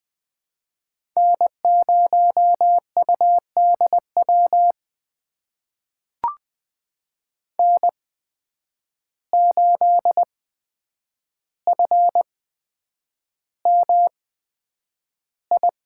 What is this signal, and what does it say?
Morse code, heard as a keyed tone, about 20 wpm, spaced as for 9 wpm, 700 Hz. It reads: N0UDW N8FMI
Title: USA Call Signs - 5 Characters - 8x Character Spacing - 20wpm